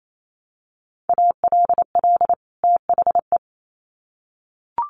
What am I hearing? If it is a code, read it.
Morse code, heard as a keyed tone, about 28 wpm, 700 Hz. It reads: ALL THE
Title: Sets of 2 Words 28wpm